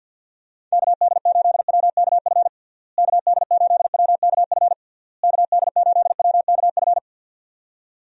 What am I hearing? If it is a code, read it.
Morse code, heard as a keyed tone, about 47 wpm, 700 Hz. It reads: KD8WKF KD8WKF KD8WKF
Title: USA Call Signs - 6 Characters - Repeated 3x - 50wpm